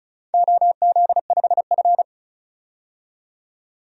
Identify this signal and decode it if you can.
Morse code, heard as a keyed tone, about 36 wpm, 700 Hz. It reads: OZ5F
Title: DXCC-100 Call Signs - 4 Characters - 35wpm